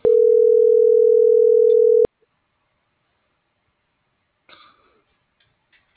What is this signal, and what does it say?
Ringback tone (North America), 440 + 480 Hz.
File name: CENTRAL NAVIERAS / out alejandro.cruz